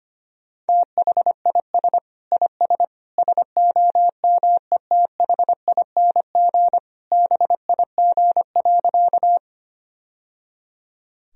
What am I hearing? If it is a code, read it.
Morse code, heard as a keyed tone, about 25 wpm, 700 Hz. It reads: THIS IS SOMETHING BIG.
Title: Sentences from Top 1000 Words - Review Part 1 - 25wpm